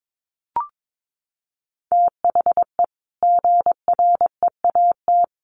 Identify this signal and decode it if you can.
Morse code, heard as a keyed tone, about 22 wpm, 700 Hz. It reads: THE GREAT
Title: Sentences from Top 1500 Words 22wpm